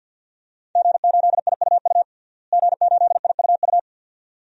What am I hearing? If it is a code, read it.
Morse code, heard as a keyed tone, about 51 wpm, 700 Hz. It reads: G8IUU G8IUU
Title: DXCC-100 Call Signs - 5 Characters - Repeated 2x - 50wpm